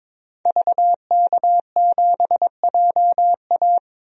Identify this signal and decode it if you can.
Morse code, heard as a keyed tone, about 22 wpm, 700 Hz. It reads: VK7JA
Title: DXCC-100 Call Signs - 5 Characters - 22wpm